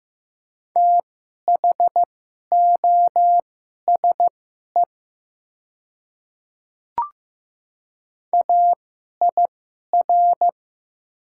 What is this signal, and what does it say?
Morse code, heard as a keyed tone, about 15 wpm, spaced as for 11 wpm, 700 Hz. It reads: THOSE AIR
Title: 2x Character Spacing - Top 300 Words - 15wpm